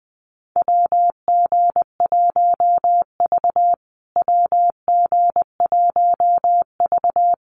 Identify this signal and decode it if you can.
Morse code, heard as a keyed tone, about 20 wpm, 700 Hz. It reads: WG1V WG1V
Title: USA Call Signs Repeated 3x 20wpm